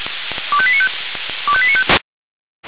Five-tone selective call (ZVEI), heard as quick runs of five tones, tones of about 70 ms. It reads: 25895 25895